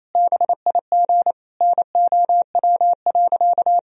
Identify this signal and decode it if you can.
Morse code, heard as a keyed tone, about 28 wpm, 700 Hz. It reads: BIG NOW.
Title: Sentences from Top 1000 Words - Review Part 1 - 28wpm